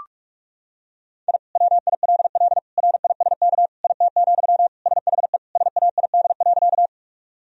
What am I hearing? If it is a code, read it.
Morse code, heard as a keyed tone, about 41 wpm, 700 Hz. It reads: I WILL RISK IT, SHE SAID.
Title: Sentences from Top 1000 Words - Review Part 4 - 45wpm